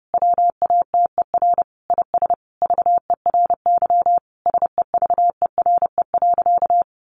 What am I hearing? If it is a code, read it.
Morse code, heard as a keyed tone, about 30 wpm, 700 Hz. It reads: WATER IS VERY SEVERE.
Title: Sentences from Top 1500 Words 30wpm